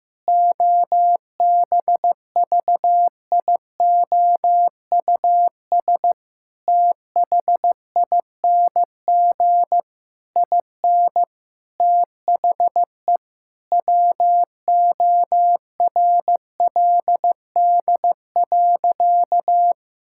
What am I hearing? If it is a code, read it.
Morse code, heard as a keyed tone, about 15 wpm, 700 Hz. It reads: OBVIOUS THING IN THE WORLD.
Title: Sentences from Top 1600 Words 15wpm